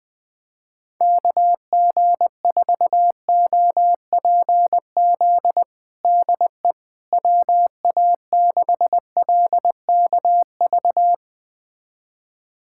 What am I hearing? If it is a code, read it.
Morse code, heard as a keyed tone, about 21 wpm, 700 Hz. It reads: KG4OPZ DE WA6LKV
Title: USA Callsign DE Callsign - 6 Characters - 20wpm